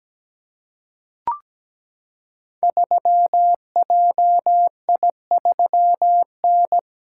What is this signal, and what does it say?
Morse code, heard as a keyed tone, about 17 wpm, 700 Hz. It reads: 3JI3N
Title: Code Groups - 5 Letter:Numbers - 17wpm